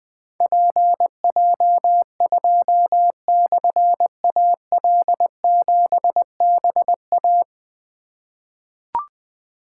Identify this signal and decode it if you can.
Morse code, heard as a keyed tone, about 20 wpm, 700 Hz. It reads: PJ2/AL7BA
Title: Difficult Call Signs 20wpm